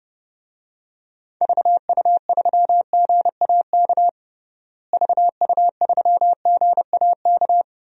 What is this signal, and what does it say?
Morse code, heard as a keyed tone, about 32 wpm, 700 Hz. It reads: VU3GAK VU3GAK
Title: DXCC-100 Call Signs - 6 Characters - Repeated 2x - 30wpm